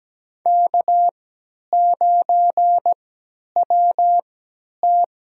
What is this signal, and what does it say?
Morse code, heard as a keyed tone, about 17 wpm, spaced as for 10 wpm, 700 Hz. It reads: K9WT
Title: USA Call Signs - 3x Character Spacing - 17wpm